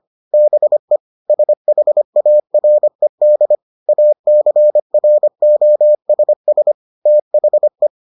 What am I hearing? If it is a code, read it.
Morse code, heard as a keyed tone, about 25 wpm, 600 Hz. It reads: BE SHARED ACROSS THE